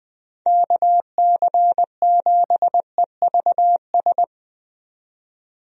Morse code, heard as a keyed tone, about 21 wpm, 700 Hz. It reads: KC7EVS
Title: USA Call Signs - 6 Characters - 20wpm